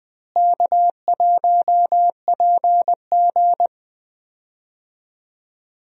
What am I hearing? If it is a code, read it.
Morse code, heard as a keyed tone, about 20 wpm, 700 Hz. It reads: K1PG
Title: USA Call Signs - 4 Characters - Repeated 3x - 20wpm